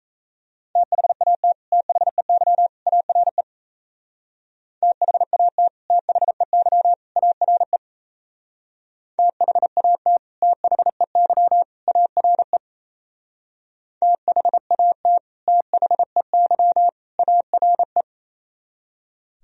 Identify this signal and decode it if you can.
Morse code, heard as a keyed tone, about 33 wpm, 700 Hz. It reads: THAT THEY ARE THAT THEY ARE THAT THEY ARE THAT THEY ARE
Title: Speed-Racing - Sets of 3 Words 28wpm